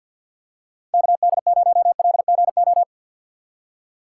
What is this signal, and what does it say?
Morse code, heard as a keyed tone, about 48 wpm, 700 Hz. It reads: KD0LGO
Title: USA Call Signs - 4x Character Spacing - 50wpm